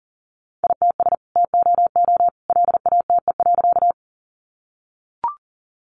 Morse code, heard as a keyed tone, about 40 wpm, 700 Hz. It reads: ITS TOO LATE.
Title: Sentences from Top 600 Words 40wpm